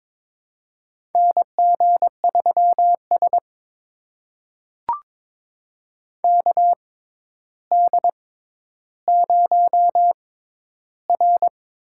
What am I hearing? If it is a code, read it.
Morse code, heard as a keyed tone, about 22 wpm, spaced as for 11 wpm, 700 Hz. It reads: NG3S KD0R
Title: USA Call Signs - 6x Character Spacing - 22wpm